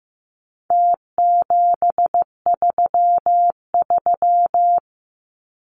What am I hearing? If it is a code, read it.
Morse code, heard as a keyed tone, about 15 wpm, 700 Hz. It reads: T733
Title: Numbers - Contest Style - 4-Digit 15wpm